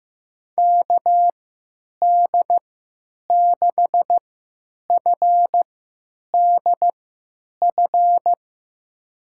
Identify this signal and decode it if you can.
Morse code, heard as a keyed tone, about 16 wpm, spaced as for 9 wpm, 700 Hz. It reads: KD6FDF
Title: USA Call Signs - 6 Characters - 3x Character Spacing - 15wpm